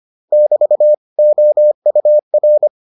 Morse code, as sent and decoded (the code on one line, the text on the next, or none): -...- / --- ..- .-.
<BT> OUR